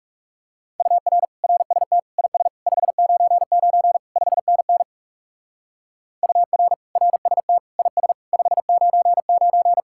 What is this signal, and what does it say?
Morse code, heard as a keyed tone, about 35 wpm, 700 Hz. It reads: UR RST IS 599 5NN UR RST IS 599
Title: Speed-Racing Common QSO Phrases - 30wpm